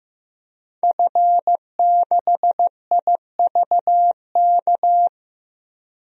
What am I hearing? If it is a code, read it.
Morse code, heard as a keyed tone, about 14 wpm, 700 Hz. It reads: F6IVK
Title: DXCC-100 Call Signs - 5 Characters - Repeated 3x - 15wpm